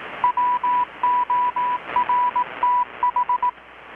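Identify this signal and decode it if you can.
Morse code, heard as a keyed tone, about 18 wpm, 990 Hz. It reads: WORTH
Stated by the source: Morse from KPH's Night of Nights XX maritime radiotelegraphy event, received in Chiapas, Mexico about 3,800 km away on a Kenwood R-600 with a grounded 10 m long-wire antenna.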